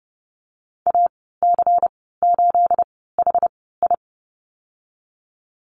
Morse code, heard as a keyed tone, about 30 wpm, spaced as for 17 wpm, 700 Hz. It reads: AC8HI
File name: Lesson 37g - Letter x - Call Signs 30wpm